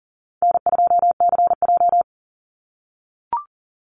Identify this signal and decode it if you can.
Morse code, heard as a keyed tone, about 40 wpm, 700 Hz. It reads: N2CJ
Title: Call Signs - 40wpm